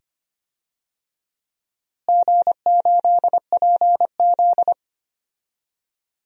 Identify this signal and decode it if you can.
Morse code, heard as a keyed tone, about 25 wpm, 700 Hz. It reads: G8PZ
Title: DXCC-100 Call Signs - 4 Characters - Repeated 2x - 25wpm